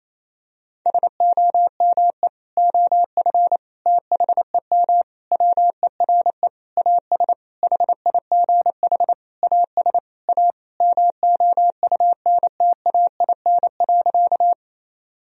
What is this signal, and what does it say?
Morse code, heard as a keyed tone, about 28 wpm, 700 Hz. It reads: SOME OF THEM WERE AS HIGH AS A MOUNTAIN.
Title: Sentences from Top 1000 Words - Review Part 1 - 28wpm